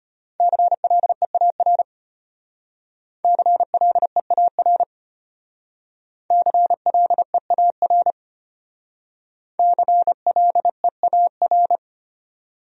Morse code, heard as a keyed tone, about 34 wpm, 700 Hz. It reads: CLEAR CLEAR CLEAR CLEAR